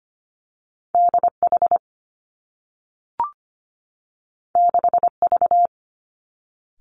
Morse code, heard as a keyed tone, about 25 wpm, 700 Hz. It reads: D H 6 V
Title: Two Challenging Letter-Number Combinations 25wpm